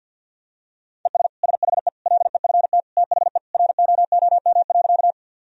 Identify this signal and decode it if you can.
Morse code, heard as a keyed tone, about 46 wpm, 700 Hz. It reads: ES SHE LEFT THE ROOM.
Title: Sentences from Top 300 Words and Abbreviations 50wpm